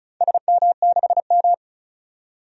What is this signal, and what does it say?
Morse code, heard as a keyed tone, about 36 wpm, 700 Hz. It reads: SM6M
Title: DXCC-100 Call Signs - 4 Characters - 35wpm